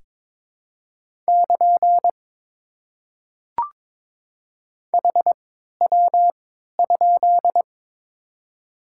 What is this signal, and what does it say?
Morse code, heard as a keyed tone, about 22 wpm, spaced as for 13 wpm, 690 Hz. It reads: <KN> HW?
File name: Top 90 QSO Elements - 3x Character Spacing - 22wpm